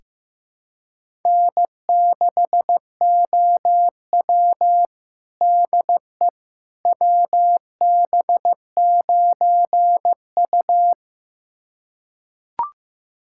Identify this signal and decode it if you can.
Morse code, heard as a keyed tone, about 16 wpm, 700 Hz. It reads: N6OW DE WB9U
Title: USA Callsign DE Callsign - 4 Characters - 15wpm